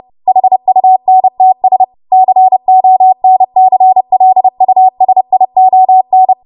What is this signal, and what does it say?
Morse code, heard as a keyed tone, about 30 wpm, 750 Hz. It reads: HUNTS CONCLUSION
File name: Morse / 30WPM